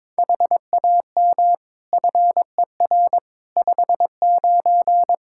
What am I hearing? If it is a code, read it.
Morse code, heard as a keyed tone, about 22 wpm, 700 Hz. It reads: HAM FER 59
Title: Practice QSO - Exchange 3 - More Info - 22wpm